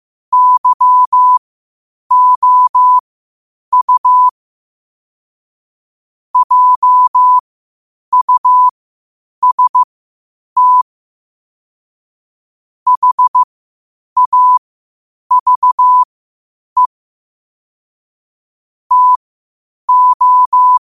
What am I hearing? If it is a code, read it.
Morse code, heard as a keyed tone, about 15 wpm, spaced as for 8 wpm, 1000 Hz. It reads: YOU JUST HAVE TO
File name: qotdfiles / 8wpm